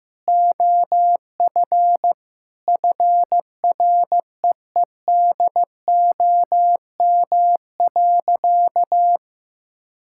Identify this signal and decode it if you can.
Morse code, heard as a keyed tone, about 16 wpm, 700 Hz. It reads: OF FREEDOM.
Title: Sentences from Top 1000 Words - Review Part 5 - 15wpm